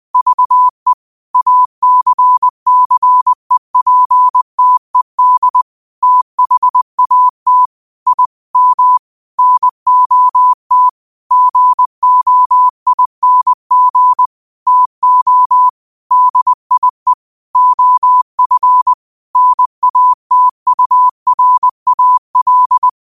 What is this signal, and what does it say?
Morse code, heard as a keyed tone, about 20 wpm, 1000 Hz. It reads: VE ACCEPTED THAT I M NOT GOING TO DIE OF NATURAL